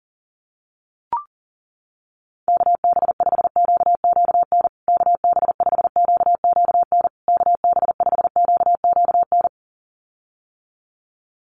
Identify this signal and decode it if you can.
Morse code, heard as a keyed tone, about 40 wpm, 700 Hz. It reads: KB5QQN KB5QQN KB5QQN
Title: USA Call Signs Repeated 3x 40wpm